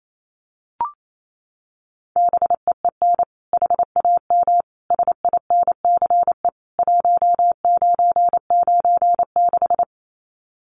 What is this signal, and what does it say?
Morse code, heard as a keyed tone, about 28 wpm, 700 Hz. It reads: BEEN HAM SINCE 1996
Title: Common QSO Phrases 28wpm